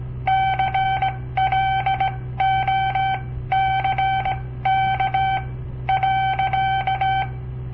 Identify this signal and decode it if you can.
Morse code, heard as a keyed tone, about 17 wpm, spaced as for 13 wpm, 770 Hz. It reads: CLOCK.
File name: papertapes / local / Code Practice-01